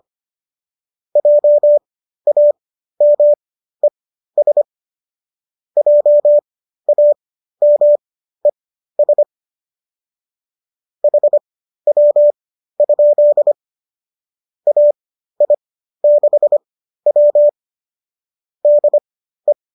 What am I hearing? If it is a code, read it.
Morse code, heard as a keyed tone, about 25 wpm, spaced as for 13 wpm, 590 Hz. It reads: JAMES JAMES HW? AI6W DE
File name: qso207 13